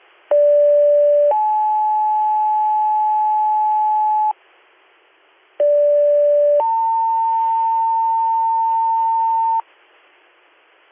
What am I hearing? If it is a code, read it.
Two-tone page, heard as a one-second tone, then a three-second tone, about 580 then 860 Hz, about 580 then 900 Hz, with noise. It is heard from 0.3 to 9.6 s.